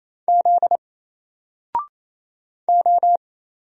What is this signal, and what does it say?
Morse code, heard as a keyed tone, about 28 wpm, 700 Hz. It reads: Z O
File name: Single-Letter-Number-28wpm